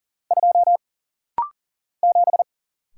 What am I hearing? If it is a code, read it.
Morse code, heard as a keyed tone, about 40 wpm, 700 Hz. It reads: 2 7
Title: Single Letter-Number - Rapid-Fire 40wpm